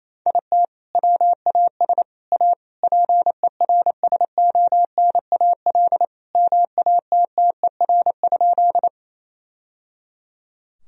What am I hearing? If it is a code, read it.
Morse code, heard as a keyed tone, about 28 wpm, 700 Hz. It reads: IT WAS A PERSONAL MATTER?
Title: Sentences from Top 1000 Words - Review Part 4 - 28wpm